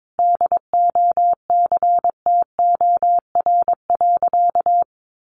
Morse code, heard as a keyed tone, about 22 wpm, 700 Hz. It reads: DOCTOR.
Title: Sentences from Top 1500 Words 22wpm